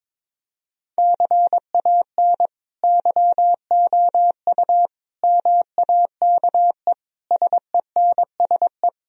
Morse code, heard as a keyed tone, about 21 wpm, 700 Hz. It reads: CAN YOU MAKE SENSE
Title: Sentences from Top 1000 Words - Review Part 4 - 22wpm